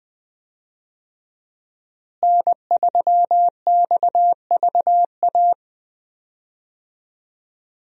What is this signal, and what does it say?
Morse code, heard as a keyed tone, about 21 wpm, 700 Hz. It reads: N3XVA
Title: USA Call Signs - 5 Characters - Repeated 3x - 20wpm